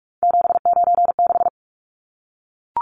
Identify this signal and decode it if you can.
Morse code, heard as a keyed tone, about 44 wpm, 700 Hz. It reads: 796
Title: Numbers - 3-Digit 45wpm